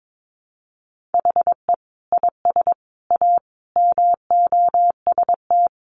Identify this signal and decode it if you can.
Morse code, heard as a keyed tone, about 22 wpm, 700 Hz. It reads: HE IS A MOST
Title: Sentences from Top 1000 Words 22wpm